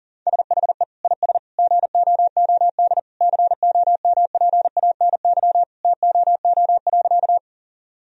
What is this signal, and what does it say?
Morse code, heard as a keyed tone, about 40 wpm, 700 Hz. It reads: SHE IS GOOD COMPANY TOO.
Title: Sentences from Top 1000 Words - Review Part 3 - 40wpm